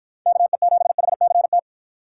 Morse code, heard as a keyed tone, about 52 wpm, 700 Hz. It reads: KE7HCT